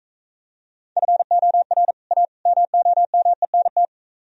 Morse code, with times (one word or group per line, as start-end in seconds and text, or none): 1.0-1.9 FOR
2.1-2.3 A
2.5-3.9 MOMENT